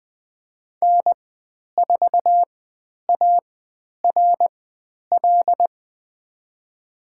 Morse code, heard as a keyed tone, about 20 wpm, spaced as for 10 wpm, 700 Hz. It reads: N4ARL
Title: 095 Lesson 16g - Letter c - Call Signs 20wpm